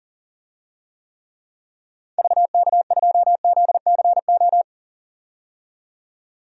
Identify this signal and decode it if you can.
Morse code, heard as a keyed tone, about 40 wpm, 700 Hz. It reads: VK2ZCO
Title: DXCC-100 Call Signs - 6 Characters - 40wpm